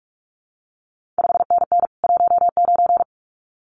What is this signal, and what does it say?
Morse code, heard as a keyed tone, about 46 wpm, 700 Hz. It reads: 5NN 19
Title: CQ WW DX Contest Exchange - 45wpm